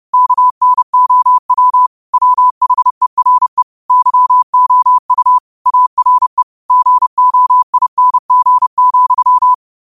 Morse code, heard as a keyed tone, about 30 wpm, 1000 Hz. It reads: KNOW WHERE YOU ARE GOING,
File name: qotdfiles / 30wpm